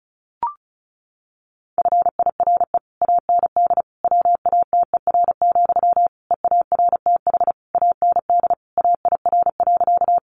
Morse code, heard as a keyed tone, about 34 wpm, 700 Hz. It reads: FIRE AND WATER, EARTH AND AIR.
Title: Sentences from Top 400 Words 35wpm